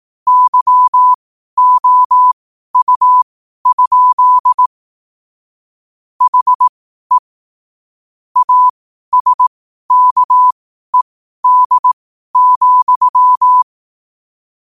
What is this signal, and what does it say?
Morse code, heard as a keyed tone, about 18 wpm, spaced as for 12 wpm, 1000 Hz. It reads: YOU? HE ASKED,
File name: qotdfiles / 12wpm